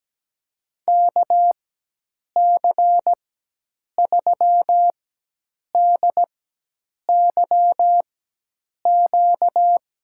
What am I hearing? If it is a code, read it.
Morse code, heard as a keyed tone, about 17 wpm, spaced as for 8 wpm, 700 Hz. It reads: KC3DYQ
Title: USA Call Signs - 4x Character Spacing - 17wpm